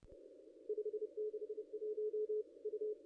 Morse code, heard as a keyed tone, about 29 wpm, 440 Hz. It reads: 561 U
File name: contests / ss-cw-k5tr-op-n5rz -0500